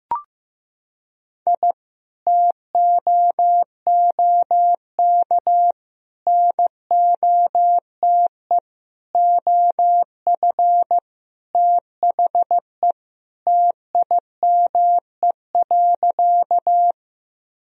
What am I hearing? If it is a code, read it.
Morse code, heard as a keyed tone, about 15 wpm, 700 Hz. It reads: I TOOK NOTE OF THE TIME.
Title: Sentences from Top 1000 Words - Review Part 2 - 15wpm